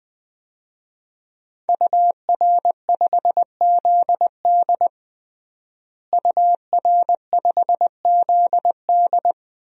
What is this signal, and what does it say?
Morse code, heard as a keyed tone, about 20 wpm, 700 Hz. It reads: UR5ZD UR5ZD
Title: DXCC-100 Call Signs - 5 Characters - Repeated 2x - 20wpm